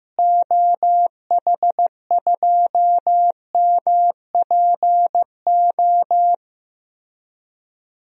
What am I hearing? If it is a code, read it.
Morse code, heard as a keyed tone, about 16 wpm, 700 Hz. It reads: OH2MPO
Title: DXCC-100 Call Signs - 6 Characters - Repeated 3x - 15wpm